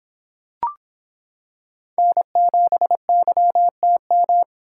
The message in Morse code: N7YTM

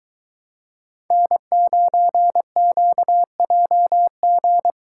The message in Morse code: N9QJG

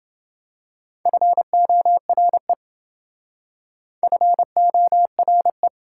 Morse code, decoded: FORE FORE

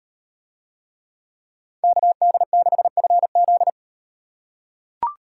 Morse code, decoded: KD6FZ

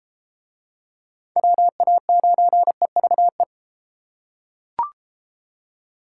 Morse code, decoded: WA9EVE

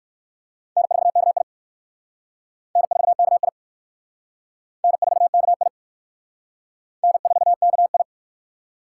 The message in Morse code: N4KI N4KI N4KI N4KI

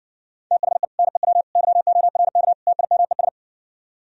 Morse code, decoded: THE NEW YORK TIMES

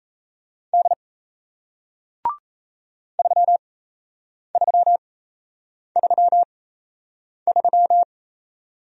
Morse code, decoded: D 3333

